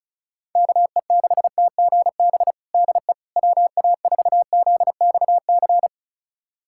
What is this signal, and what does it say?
Morse code, heard as a keyed tone, about 34 wpm, 700 Hz. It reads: KE6TGB DE WA4ZXC
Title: USA Callsign DE Callsign 35wpm